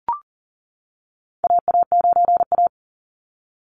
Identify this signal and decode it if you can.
Morse code, heard as a keyed tone, about 40 wpm, 700 Hz. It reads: AA9A
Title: Call Signs - 40wpm